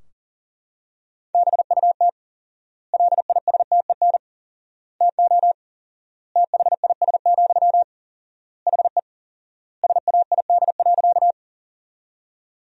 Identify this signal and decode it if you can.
Morse code, heard as a keyed tone, about 40 wpm, 700 Hz. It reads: BUT LISTEN TO THIS, HE SAID.
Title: 4x Word-Spacing - Sentences from Top 400 Words - 40wpm